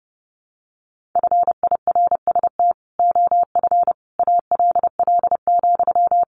Morse code, heard as a keyed tone, about 30 wpm, 700 Hz. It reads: FIRST OF ALL,
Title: Sentences from Top 1500 Words 30wpm